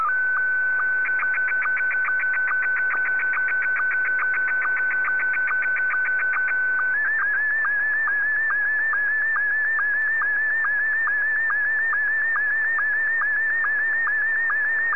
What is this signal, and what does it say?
SSTV (slow-scan television), Scottie 1.
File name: VK7ZCF